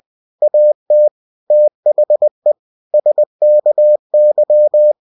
Morse code, heard as a keyed tone, about 20 wpm, 600 Hz. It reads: AT THE SKY
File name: cw-books / winnie-the-pooh / winnie-20wpm-page-0035